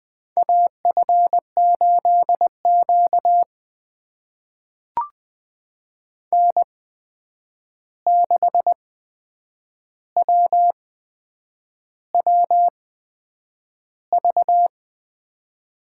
Morse code, heard as a keyed tone, about 20 wpm, spaced as for 8 wpm, 700 Hz. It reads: AF8Q N6WWV
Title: USA Call Signs - 8x Character Spacing - 20wpm